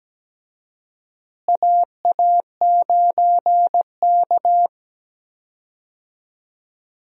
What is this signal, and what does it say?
Morse code, heard as a keyed tone, about 17 wpm, 700 Hz. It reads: AA9K
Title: USA Call Signs - 4 Characters - Repeated 2x - 17wpm